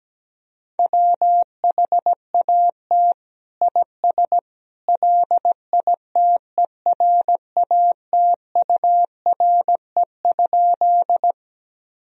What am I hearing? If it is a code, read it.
Morse code, heard as a keyed tone, about 17 wpm, 700 Hz. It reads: WHAT IS LITERATURE?
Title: Sentences from Top 1000 Words - Review Part 3 - 17wpm